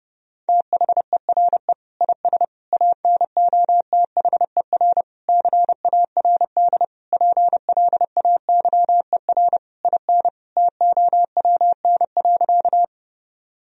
Morse code, heard as a keyed tone, about 30 wpm, 700 Hz. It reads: THERE IS ANOTHER CARD PLAYER IN TOWN.
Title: Sentences from Top 1000 Words - Review Part 4 - 30wpm